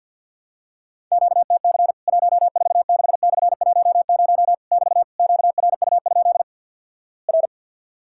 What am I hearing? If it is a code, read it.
Morse code, heard as a keyed tone, about 50 wpm, 700 Hz. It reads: QTC 146/10 <BT> QRU? R
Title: Worked All Europe DX Contest - QTC Exchange - 50wpm